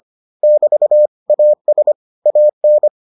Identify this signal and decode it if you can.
Morse code, heard as a keyed tone, about 25 wpm, 600 Hz. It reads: <BT> AS AN